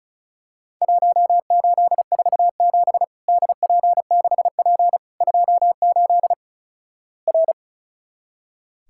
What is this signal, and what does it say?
Morse code, heard as a keyed tone, about 36 wpm, 700 Hz. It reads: 1847 DP6P 28 R
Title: Worked All Europe DX Contest - QTC Exchange - 35wpm